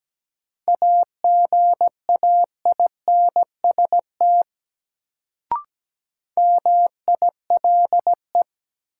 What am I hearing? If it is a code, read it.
Morse code, heard as a keyed tone, about 17 wpm, 700 Hz. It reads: AGAINST MILE
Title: Sentences from Top 400 Words - New Word Drill - 17wpm